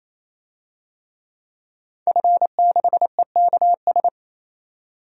Morse code, heard as a keyed tone, about 28 wpm, 700 Hz. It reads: F6EKS
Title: DXCC-100 Call Signs - 5 Characters - 28wpm